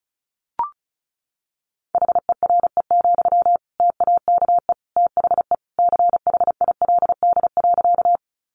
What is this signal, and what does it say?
Morse code, heard as a keyed tone, about 34 wpm, 700 Hz. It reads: HERE, TAKE THE CHILD.
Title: Sentences from Top 800 Words 35wpm